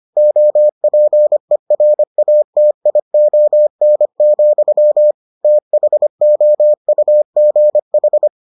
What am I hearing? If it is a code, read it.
Morse code, heard as a keyed tone, about 25 wpm, 600 Hz. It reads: OPERATION, THOUGH